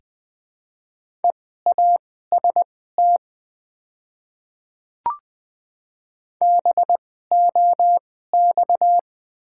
Morse code, heard as a keyed tone, about 20 wpm, spaced as for 14 wpm, 690 Hz. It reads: EAST BOX
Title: 2x Character Spacing - Top 500 Words - 20wpm